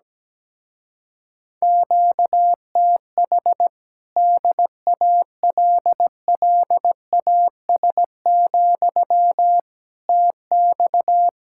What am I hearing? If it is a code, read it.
Morse code, heard as a keyed tone, about 17 wpm, 700 Hz. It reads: QTH DALLAS, TX
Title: Speed-Racing Common QSO Phrases Encore - 17wpm